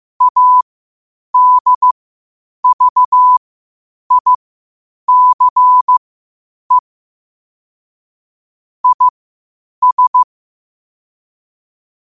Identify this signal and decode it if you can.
Morse code, heard as a keyed tone, about 15 wpm, spaced as for 8 wpm, 1000 Hz. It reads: ADVICE IS
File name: qotdfiles / 8wpm